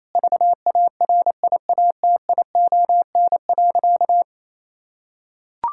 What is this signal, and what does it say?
Morse code, heard as a keyed tone, about 28 wpm, 700 Hz. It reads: VARIATION.
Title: Sentences from Top 1600 Words 28wpm